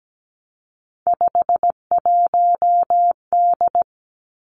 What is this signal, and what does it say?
Morse code, heard as a keyed tone, about 17 wpm, 700 Hz. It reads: 51D
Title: Three Challenging Letter-Number Combinations 17wpm